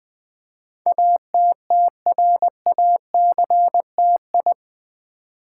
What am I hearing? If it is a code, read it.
Morse code, heard as a keyed tone, about 20 wpm, 700 Hz. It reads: ATTRACTI
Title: Build A Word - 4 Letters At A Time - 10-Letter Words - 20wpm